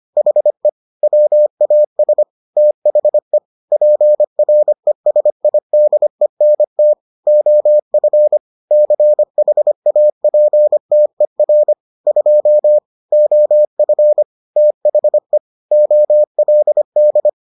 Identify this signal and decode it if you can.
Morse code, heard as a keyed tone, about 25 wpm, 600 Hz. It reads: HE WAS THE PRESIDENT OF CHAPTER 2 OF THE OLD